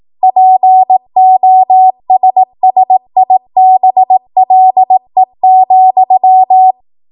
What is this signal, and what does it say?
Morse code, heard as a keyed tone, about 18 wpm, 750 Hz. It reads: POSSIBLE,
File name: Morse / 18WPM